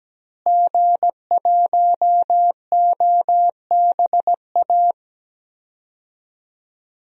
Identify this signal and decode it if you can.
Morse code, heard as a keyed tone, about 17 wpm, 700 Hz. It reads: G1OBA
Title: DXCC-100 Call Signs - 5 Characters - Repeated 3x - 17wpm